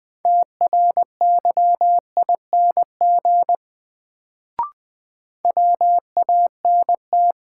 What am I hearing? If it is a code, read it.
Morse code, heard as a keyed tone, about 20 wpm, 700 Hz. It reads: TRYING WANT
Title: Sets of 2 Words - New Word Drill - 20wpm